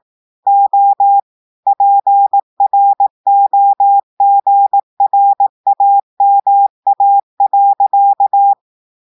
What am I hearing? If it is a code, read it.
Morse code, heard as a keyed tone, about 18 wpm, 800 Hz. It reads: O PROGRAMA.
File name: audios / portugais / portugais texte en clair 2 vitesse 18 mots minutes